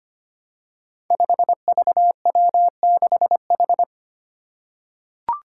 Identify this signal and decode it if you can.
Morse code, heard as a keyed tone, about 25 wpm, 700 Hz. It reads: 5VW6H
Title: Five Challenging Letter-Number Combinations 25wpm